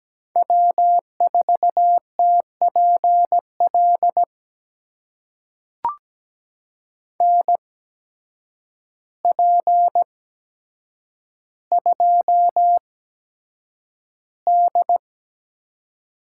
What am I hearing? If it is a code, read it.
Morse code, heard as a keyed tone, about 17 wpm, spaced as for 8 wpm, 700 Hz. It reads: W4TPL NP2D
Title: USA Call Signs - 5 Characters - 8x Character Spacing - 17wpm